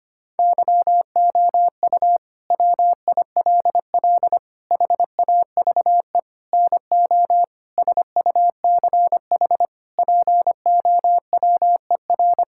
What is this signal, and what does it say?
Morse code, heard as a keyed tone, about 25 wpm, 700 Hz. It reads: YOU WILL HAVE NO SUCH POWER
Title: Sentences from Top 1000 Words - Review Part 6 - 25wpm